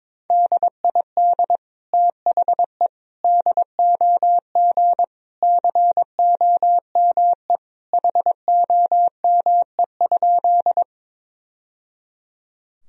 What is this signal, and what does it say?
Morse code, heard as a keyed tone, about 22 wpm, 700 Hz. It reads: DID THE DOG COME HOME?
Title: Sentences from Top 1000 Words - Review Part 2 - 22wpm